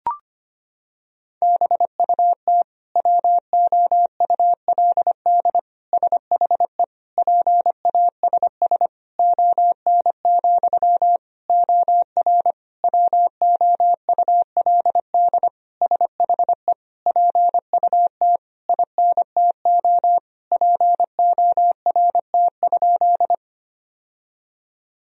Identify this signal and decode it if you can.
Morse code, heard as a keyed tone, about 25 wpm, 700 Hz. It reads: BUT WOULD SHE PASS ON, OR WOULD SHE PUT INTO PORT?
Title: Sentences from Top 1000 Words - Review Part 1 - 25wpm